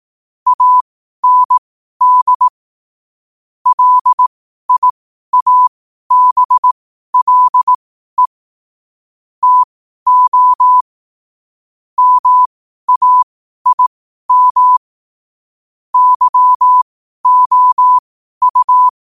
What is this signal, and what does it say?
Morse code, heard as a keyed tone, about 18 wpm, spaced as for 12 wpm, 1000 Hz. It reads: AND LIABLE TO MAIM YOU
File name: qotdfiles / 12wpm